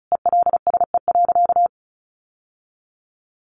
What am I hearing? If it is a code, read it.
Morse code, heard as a keyed tone, about 34 wpm, 700 Hz. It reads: ELSE.
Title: Sentences from Top 800 Words 35wpm